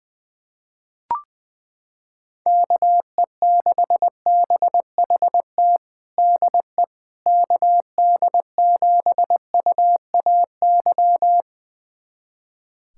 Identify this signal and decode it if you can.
Morse code, heard as a keyed tone, about 20 wpm, 700 Hz. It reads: KE6BHT DE KD7UAY
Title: USA Callsign DE Callsign 20wpm